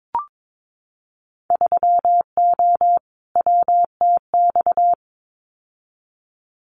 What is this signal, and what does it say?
Morse code, heard as a keyed tone, about 22 wpm, 700 Hz. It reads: 3O WTX
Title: Winter Field Day Exchanges Encore 22wpm